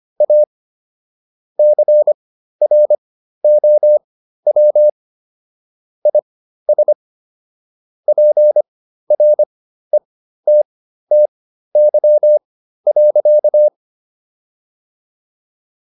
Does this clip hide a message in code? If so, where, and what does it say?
Yes: Morse code from 0.2 to 13.7 s, heard as a keyed tone, about 25 wpm, spaced as for 13 wpm, 600 Hz. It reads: A CROW IS PRETTY.